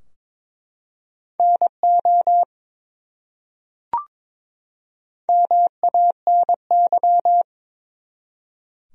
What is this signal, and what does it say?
Morse code, heard as a keyed tone, about 22 wpm, 700 Hz. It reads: NO MANY
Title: Top 100 Words 22wpm